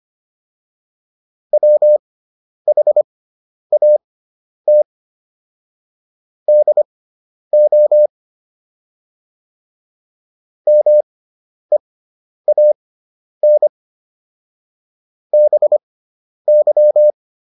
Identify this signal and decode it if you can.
Morse code, heard as a keyed tone, about 25 wpm, spaced as for 10 wpm, 600 Hz. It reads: WHAT DO MEAN BY